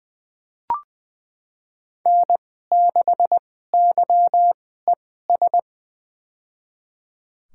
Morse code, heard as a keyed tone, about 21 wpm, spaced as for 15 wpm, 700 Hz. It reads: N6YES
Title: USA Call Signs - 5 Characters - 2x Character Spacing - 20wpm